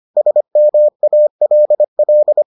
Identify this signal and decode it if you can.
Morse code, heard as a keyed tone, about 25 wpm, 600 Hz. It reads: SMALL